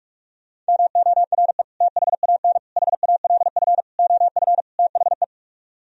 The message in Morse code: MORE THAN HALF OF THE

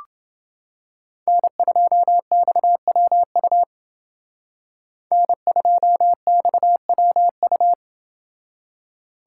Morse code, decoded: N2XWU N2XWU

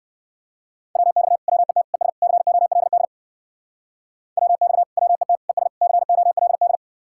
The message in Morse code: WX WET ES COLD WX WET ES COLD

